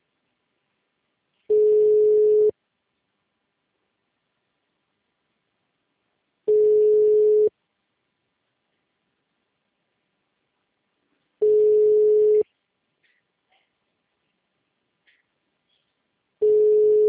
Ringback tone (Europe, 425 Hz).
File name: out- -2026- - - .39110